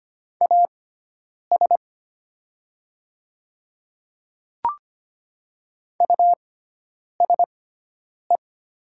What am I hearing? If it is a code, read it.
Morse code, heard as a keyed tone, about 25 wpm, spaced as for 9 wpm, 700 Hz. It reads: AS USE
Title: 6x Character Spacing - Top 200 Words - 25wpm